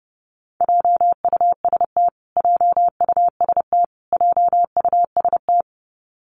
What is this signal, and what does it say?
Morse code, heard as a keyed tone, about 30 wpm, 700 Hz. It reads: JUST JUST JUST
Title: Top 500 Words Repeated 30wpm